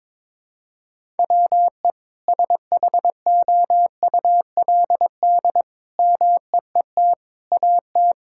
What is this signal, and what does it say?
Morse code, heard as a keyed tone, about 22 wpm, 700 Hz. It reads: WE SHOULD MEET AT
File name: Sentences with 7-Word Phrases - 22wpm